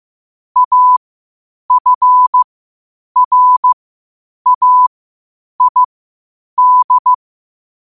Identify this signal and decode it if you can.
Morse code, heard as a keyed tone, about 15 wpm, spaced as for 8 wpm, 1000 Hz. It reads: AFRAID
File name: qotdfiles / 8wpm